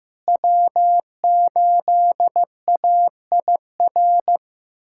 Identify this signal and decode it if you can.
Morse code, heard as a keyed tone, about 16 wpm, 700 Hz. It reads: W8AIR
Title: USA Callsign DE Callsign 15wpm